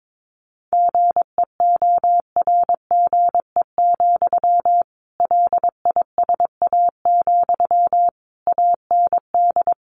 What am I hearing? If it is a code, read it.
Morse code, heard as a keyed tone, about 22 wpm, 700 Hz. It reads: GEORGE, LISA, AND
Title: Additional Easy Phrases 22wpm